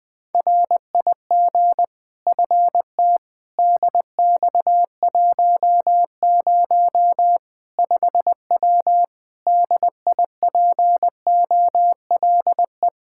Morse code, heard as a keyed tone, about 21 wpm, 700 Hz. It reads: RIG FT DX10 5W DIPOLE
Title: Practice QSO - Exchange 2 - Info - 20wpm